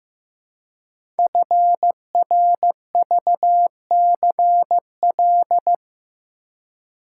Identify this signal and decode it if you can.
Morse code, heard as a keyed tone, about 15 wpm, 700 Hz. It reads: FRVCL
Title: Code Groups - 5 Letters - 15wpm